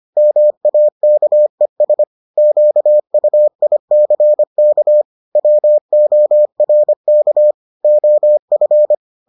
Morse code, heard as a keyed tone, about 25 wpm, 600 Hz. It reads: MAKES QUICK WORK OF